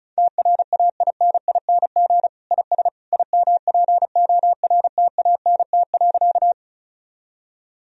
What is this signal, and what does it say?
Morse code, heard as a keyed tone, about 36 wpm, 700 Hz. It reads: TRAINING IS IMPORTANT.
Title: Sentences from Top 1000 Words - Review Part 4 - 35wpm